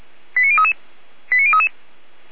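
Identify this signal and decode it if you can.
Five-tone selective call (ZVEI), heard as quick runs of five tones, tones of about 70 ms. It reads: 89033 89033
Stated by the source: ZVEI five-tone call 89079 received by SDR.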